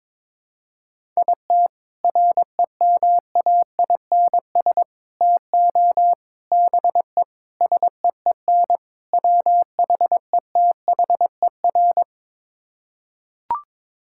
Morse code, heard as a keyed tone, about 22 wpm, 700 Hz. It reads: IT REMAINS TO BE SEEN WHETHER
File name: practice / 22wpm / Sets of 6 Words - 22wpm